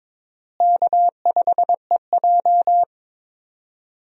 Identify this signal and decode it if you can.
Morse code, heard as a keyed tone, about 22 wpm, 700 Hz. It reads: K5EJ
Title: USA Call Signs - 4 Characters - Repeated 3x - 22wpm